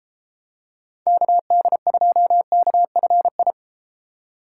Morse code, heard as a keyed tone, about 32 wpm, 700 Hz. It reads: KD2KFI